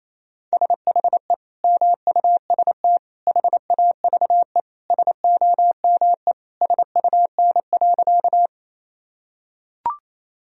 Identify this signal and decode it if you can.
Morse code, heard as a keyed tone, about 28 wpm, 700 Hz. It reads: SHE MUST HAVE SOME SUN.
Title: Sentences from Top 1000 Words - Review Part 1 - 28wpm